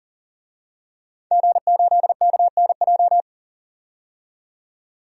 Morse code, heard as a keyed tone, about 40 wpm, 700 Hz. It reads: G8KNJ
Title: DXCC-100 Call Signs - 5 Characters - 40wpm